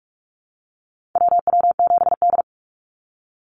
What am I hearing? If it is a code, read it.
Morse code, heard as a keyed tone, about 44 wpm, 700 Hz. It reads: WW7D
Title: NA SOTA Chaser Callsigns 45wpm